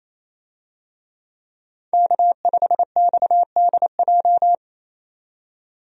Morse code, heard as a keyed tone, about 28 wpm, 700 Hz. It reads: K5XDJ
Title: USA Call Signs - 5 Characters - 28wpm